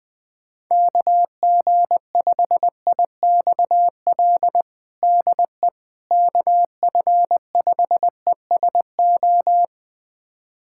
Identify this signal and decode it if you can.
Morse code, heard as a keyed tone, about 20 wpm, 700 Hz. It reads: KG5IXL DE KF5ESO
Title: USA Callsign DE Callsign - 6 Characters - 20wpm